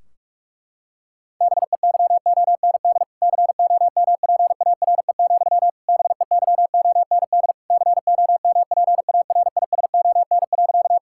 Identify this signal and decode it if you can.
Morse code, heard as a keyed tone, about 46 wpm, 700 Hz. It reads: BEYOND COMPARE, BEYOND COMPARISON.
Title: Sentences from Top 1600 Words 45wpm